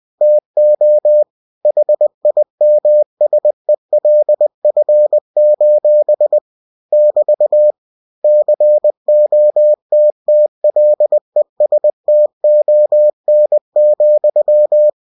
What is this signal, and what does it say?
Morse code, heard as a keyed tone, about 20 wpm, 600 Hz. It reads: TO HIMSELF: <BT> COTTLESTON,
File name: cw-books / winnie-the-pooh / winnie-20wpm-page-0046